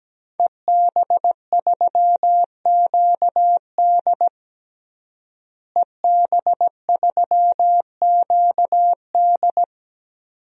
Morse code, heard as a keyed tone, about 17 wpm, 700 Hz. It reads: EB3QD EB3QD
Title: DXCC-100 Call Signs - 5 Characters - Repeated 2x - 17wpm